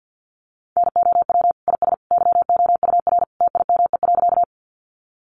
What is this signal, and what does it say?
Morse code, heard as a keyed tone, about 50 wpm, 700 Hz. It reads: NOW IS YOUR TIME.
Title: Sentences from Top 100 Words 50wpm